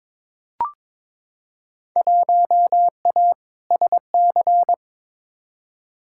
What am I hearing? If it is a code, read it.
Morse code, heard as a keyed tone, about 23 wpm, 700 Hz. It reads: 1A SC
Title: Field Day Exchanges - 22wpm